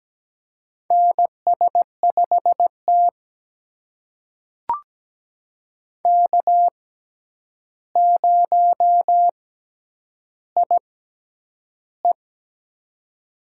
Morse code, heard as a keyed tone, about 17 wpm, spaced as for 9 wpm, 700 Hz. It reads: NS5T K0IE
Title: USA Call Signs - 4 Characters - 6x Character Spacing - 17wpm